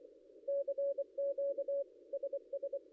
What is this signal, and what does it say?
Morse code, heard as a keyed tone, about 24 wpm, 560 Hz. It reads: CQ SS
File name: contests / ss-cw-k5tr-op-n5rz -0800